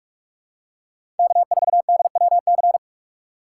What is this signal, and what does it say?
Morse code, heard as a keyed tone, about 46 wpm, 700 Hz. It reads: K4DWC
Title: USA Call Signs - 5 Characters - 3x Character Spacing - 45wpm